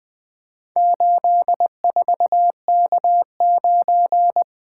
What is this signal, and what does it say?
Morse code, heard as a keyed tone, about 20 wpm, 700 Hz. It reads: 84K9
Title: Four Letter-Number Combinations - 20wpm